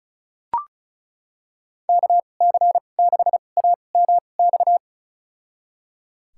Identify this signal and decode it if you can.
Morse code, heard as a keyed tone, about 34 wpm, spaced as for 25 wpm, 700 Hz. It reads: KC6AMX
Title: USA Call Signs - 2x Character Spacing - 35wpm